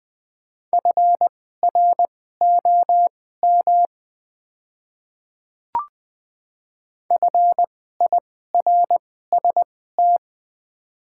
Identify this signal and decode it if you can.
Morse code, heard as a keyed tone, about 20 wpm, spaced as for 14 wpm, 700 Hz. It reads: FROM FIRST
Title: 2x Character Spacing - Top 100 Words - 20wpm